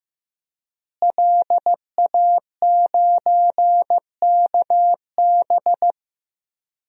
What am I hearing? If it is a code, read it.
Morse code, heard as a keyed tone, about 15 wpm, 700 Hz. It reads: LA9KB
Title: DXCC-100 Call Signs - 5 Characters - Repeated 3x - 15wpm